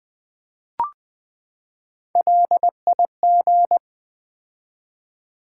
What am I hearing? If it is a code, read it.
Morse code, heard as a keyed tone, about 20 wpm, 700 Hz. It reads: LIG
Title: Build A Word - 3 Letters At A Time - 8-Letter Words - 20wpm